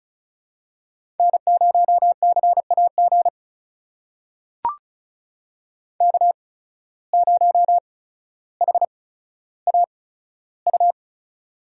Morse code, heard as a keyed tone, about 36 wpm, spaced as for 15 wpm, 700 Hz. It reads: N0CAG K0HAU
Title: USA Call Signs - 5 Characters - 8x Character Spacing - 35wpm